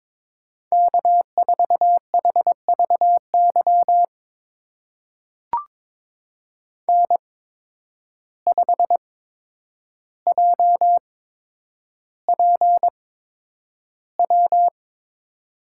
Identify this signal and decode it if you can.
Morse code, heard as a keyed tone, about 22 wpm, spaced as for 9 wpm, 700 Hz. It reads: K4HVY N5JPW
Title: USA Call Signs - 5 Characters - 8x Character Spacing - 22wpm